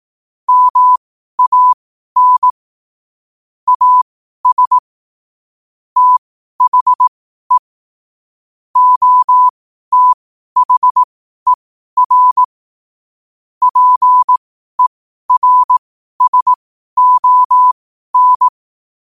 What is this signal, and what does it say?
Morse code, heard as a keyed tone, about 18 wpm, spaced as for 12 wpm, 1000 Hz. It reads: MAN AS THE OTHER PERSON
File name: qotdfiles / 12wpm